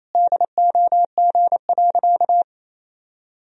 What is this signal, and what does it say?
Morse code, heard as a keyed tone, about 28 wpm, 700 Hz. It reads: DOG.
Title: Sentences from Top 1000 Words - Review Part 2 - 28wpm